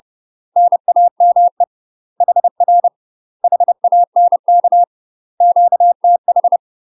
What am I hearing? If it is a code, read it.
Morse code, heard as a keyed tone, about 30 wpm, 700 Hz. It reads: NAME HR HANK QTH
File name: UK MASTER QSOs TRAINING / 30WPM / UK MASTER QSOs 0042